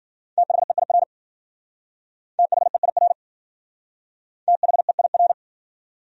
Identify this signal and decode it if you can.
Morse code, heard as a keyed tone, about 50 wpm, 700 Hz. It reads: THEIR THEIR THEIR
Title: Speed-Racing Top 500 Words 40wpm